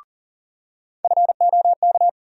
Morse code, heard as a keyed tone, about 40 wpm, 700 Hz. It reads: FOK
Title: USA Call Sign Postfixes - 3 Letters 40wpm